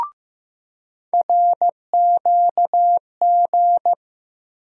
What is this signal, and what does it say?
Morse code, heard as a keyed tone, about 15 wpm, 700 Hz. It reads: RQG
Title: USA Call Sign Postfixes - 3 Letters 15wpm